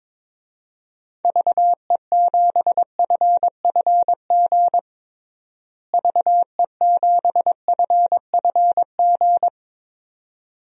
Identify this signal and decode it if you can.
Morse code, heard as a keyed tone, about 21 wpm, 700 Hz. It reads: VE7FFG VE7FFG
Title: DXCC-100 Call Signs - 6 Characters - Repeated 2x - 22wpm